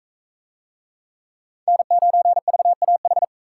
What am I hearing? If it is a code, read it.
Morse code, heard as a keyed tone, about 42 wpm, 700 Hz. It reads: N9VAH